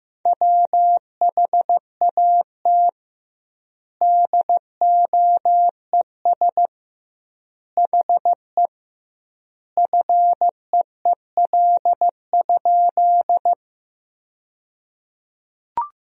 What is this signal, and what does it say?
Morse code, heard as a keyed tone, about 15 wpm, 700 Hz. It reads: WHAT DOES HE FEEL?
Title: 2x Word-Spacing - Sentences from Top 400 Words - 15wpm